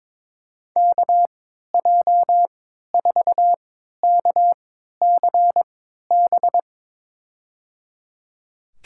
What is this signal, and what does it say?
Morse code, heard as a keyed tone, about 22 wpm, spaced as for 12 wpm, 700 Hz. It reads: KJ4KCB
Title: USA Call Signs - 6 Characters - 3x Character Spacing - 22wpm